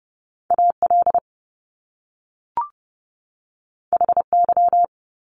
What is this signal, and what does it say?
Morse code, heard as a keyed tone, about 30 wpm, 700 Hz. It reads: AL HY
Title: Top Two Letter Combinations in Words 30wpm